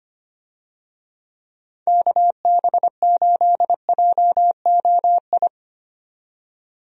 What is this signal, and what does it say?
Morse code, heard as a keyed tone, about 25 wpm, 700 Hz. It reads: KB8JOI
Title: USA Call Signs - 6 Characters - 25wpm